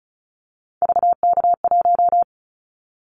Morse code, heard as a keyed tone, about 36 wpm, 700 Hz. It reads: VK1
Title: Callsign Prefixes for Top 100 DXCC - 35wpm